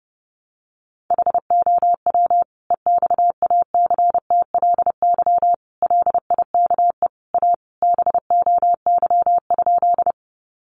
Morse code, heard as a keyed tone, about 30 wpm, 700 Hz. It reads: HOW EXACTLY LIKE A BOY?
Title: Sentences from Top 900 Words 30wpm